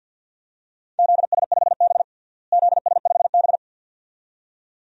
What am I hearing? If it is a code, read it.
Morse code, heard as a keyed tone, about 52 wpm, 700 Hz. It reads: ZS5B ZS5B
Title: DXCC-100 Call Signs - 4 Characters - Repeated 2x - 50wpm